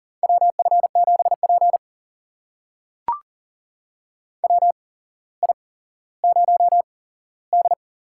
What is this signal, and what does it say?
Morse code, heard as a keyed tone, about 42 wpm, spaced as for 17 wpm, 690 Hz. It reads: WF7P WI0D
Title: USA Call Signs - 4 Characters - 8x Character Spacing - 40wpm